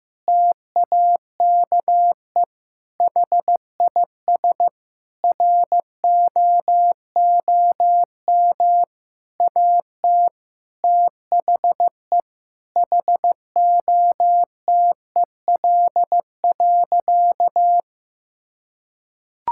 Morse code, heard as a keyed tone, about 15 wpm, 700 Hz. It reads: TAKE HIS ROOM AT THE HOTEL.
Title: Sentences from Top 1600 Words 15wpm